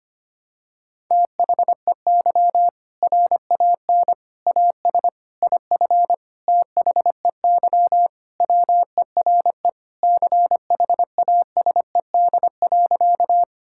Morse code, heard as a keyed tone, about 25 wpm, 700 Hz. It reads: THEY RAN AS IF THEY WERE CHASED.
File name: practice / 25wpm / Sentences with 4-Word Phrases - 25wpm